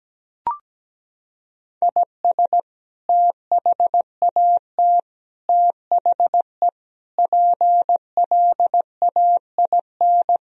Morse code, heard as a keyed tone, about 17 wpm, 700 Hz. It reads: IS THAT THE PLAIN
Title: Sentences from Top 1000 Words - Review Part 2 - 17wpm